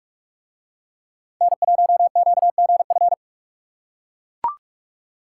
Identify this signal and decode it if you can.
Morse code, heard as a keyed tone, about 40 wpm, 700 Hz. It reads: N1QGF